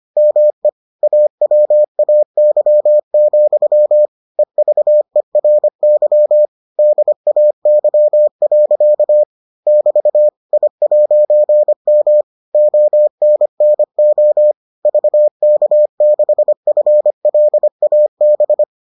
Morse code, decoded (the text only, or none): ME AWAY, EVERY DAY. <BT> I'M ONNO VK6FLAB